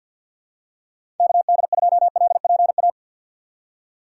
Morse code, heard as a keyed tone, about 47 wpm, 700 Hz. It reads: KD2LPA
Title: USA Call Signs - 6 Characters - 6x Character Spacing - 50wpm